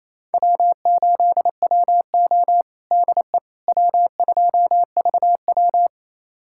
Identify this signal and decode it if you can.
Morse code, heard as a keyed tone, about 28 wpm, 700 Hz. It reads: W8WO DE W2VW
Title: USA Callsign DE Callsign - 4 Characters - 28wpm